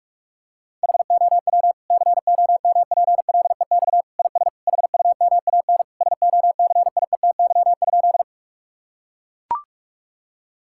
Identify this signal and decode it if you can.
Morse code, heard as a keyed tone, about 41 wpm, 700 Hz. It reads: HOW COMPLEX IS HUMAN SOCIETY?
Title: Sentences from Top 1000 Words - Review Part 6 - 45wpm